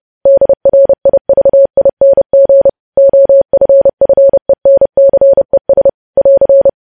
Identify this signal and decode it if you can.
Morse code, heard as a keyed tone, about 30 wpm, 550 Hz. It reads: DRIVING OFFENCES <AR>